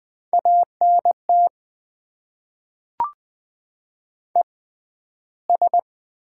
Morse code, heard as a keyed tone, about 20 wpm, spaced as for 12 wpm, 690 Hz. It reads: ANT ES
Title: Top 20 QSO Elements - 6x Character Spacing - 20wpm